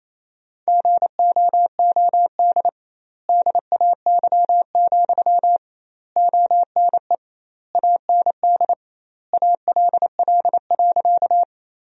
Morse code, heard as a keyed tone, about 28 wpm, 700 Hz. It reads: GOOD DAY, ONE AND ALL.
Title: 2x Word-Spacing - Sentences from Top 200 Words - 28wpm